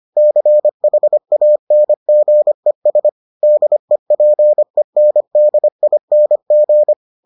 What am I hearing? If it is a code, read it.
Morse code, heard as a keyed tone, about 25 wpm, 600 Hz. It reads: CHANGES DEPENDING